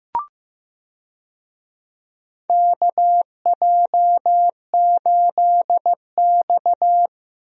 Morse code, heard as a keyed tone, about 15 wpm, 700 Hz. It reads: KJ8X
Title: USA Call Signs - 4 Characters - Repeated 3x - 15wpm